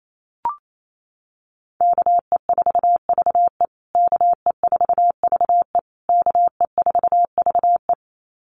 Morse code, heard as a keyed tone, about 28 wpm, 690 Hz. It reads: KE4VE KE4VE KE4VE
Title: USA Call Signs Repeated 3x 28wpm